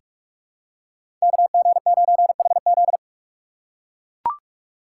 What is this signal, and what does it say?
Morse code, heard as a keyed tone, about 44 wpm, 700 Hz. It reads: KG9HZ
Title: USA Call Signs - 5 Characters - 2x Character Spacing - 45wpm